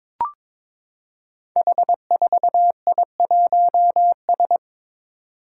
Morse code, heard as a keyed tone, about 21 wpm, 700 Hz. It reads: H4I1S
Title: Five Challenging Letter-Number Combinations 22wpm